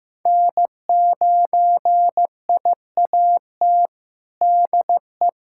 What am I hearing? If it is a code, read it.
Morse code, heard as a keyed tone, about 15 wpm, 700 Hz. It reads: N9IAT DE
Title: USA Callsign DE Callsign - 5 Characters - 15wpm